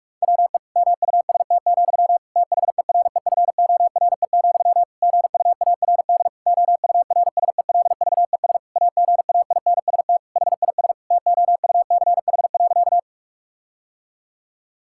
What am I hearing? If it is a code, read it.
Morse code, heard as a keyed tone, about 46 wpm, 700 Hz. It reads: WE MUST, THEREFORE, GUARD OURSELVES AGAINST HIS TOUCH.
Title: Sentences from Top 1600 Words 45wpm